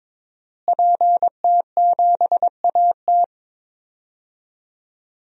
Morse code, heard as a keyed tone, about 22 wpm, 700 Hz. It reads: PT7AT
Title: DXCC-100 Call Signs - 5 Characters - Repeated 2x - 22wpm